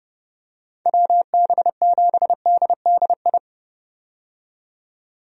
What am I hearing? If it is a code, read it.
Morse code, heard as a keyed tone, about 30 wpm, 700 Hz. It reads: WB7DDI
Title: USA Call Signs - 6 Characters - 30wpm